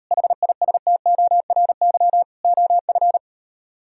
Morse code, heard as a keyed tone, about 38 wpm, 690 Hz. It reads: HISTORY OF